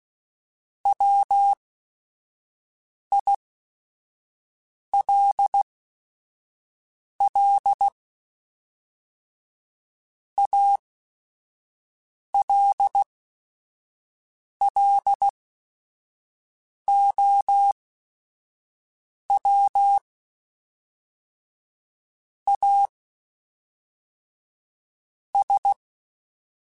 Morse code, heard as a keyed tone, about 16 wpm, spaced as for 5 wpm, 800 Hz. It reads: WILL ALLOW AS
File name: Technical Subjects / Technical Subjects-mp3s / Sw Pwr Sup @5